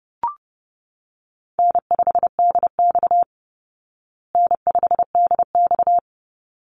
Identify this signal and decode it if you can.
Morse code, heard as a keyed tone, about 30 wpm, 700 Hz. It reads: N5DX N5DX
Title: Call Signs Repeated 3x - 30wpm